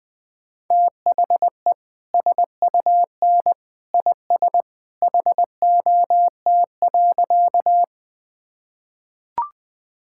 Morse code, heard as a keyed tone, about 20 wpm, 700 Hz. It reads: THE SUN IS HOT.
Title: Easy Sentences with Hints - 20wpm